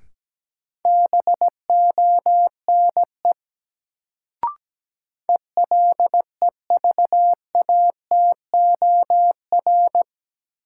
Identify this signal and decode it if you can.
Morse code, heard as a keyed tone, about 17 wpm, 700 Hz. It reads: BONE ELEVATOR
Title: Sentences from Top 1600 Words - New Word Drill - 17wpm